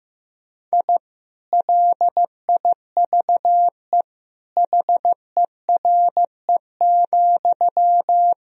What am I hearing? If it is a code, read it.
Morse code, heard as a keyed tone, about 15 wpm, 700 Hz. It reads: I LIVE HERE,
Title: Sentences from Top 1000 Words - Review Part 1 - 15wpm